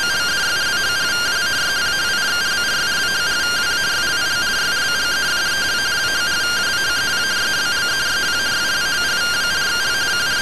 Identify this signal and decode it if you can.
Two-tone teleprinter signal, about 170 Hz shift.
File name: findings / 202 coronal iodine 3mm.gen-00 rtty rx